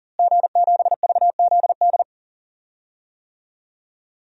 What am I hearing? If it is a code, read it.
Morse code, heard as a keyed tone, about 40 wpm, 700 Hz. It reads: G7VZD
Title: DXCC-100 Call Signs - 5 Characters - Repeated 3x - 40wpm